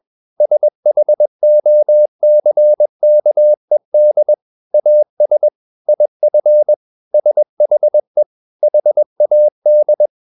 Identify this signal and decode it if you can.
Morse code, heard as a keyed tone, about 21 wpm, 600 Hz. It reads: SHOCKED AS IF SHE HAD
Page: CW med Gnister